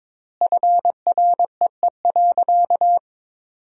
Morse code, heard as a keyed tone, about 22 wpm, 700 Hz. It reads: FREE.
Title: 4x Word-Spacing - Sentences from Top 500 Words - 22wpm